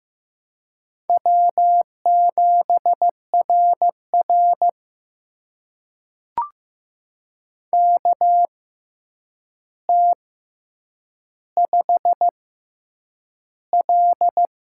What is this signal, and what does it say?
Morse code, heard as a keyed tone, about 15 wpm, spaced as for 8 wpm, 700 Hz. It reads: W7RR KT5L
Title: USA Call Signs - 4 Characters - 6x Character Spacing - 15wpm